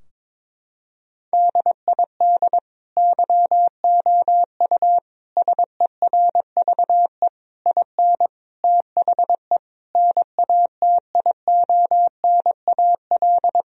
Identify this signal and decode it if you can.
Morse code, heard as a keyed tone, about 22 wpm, 700 Hz. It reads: DID YOU SERVE IN THE NATIONAL
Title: Sentences from Top 1000 Words - Review Part 4 - 22wpm